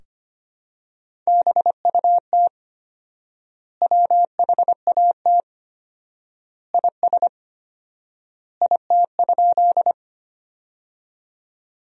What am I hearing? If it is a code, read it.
Morse code, heard as a keyed tone, about 25 wpm, 700 Hz. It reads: BUT WHAT IS IT?
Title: 4x Word-Spacing - Sentences from Top 100 Words - 25wpm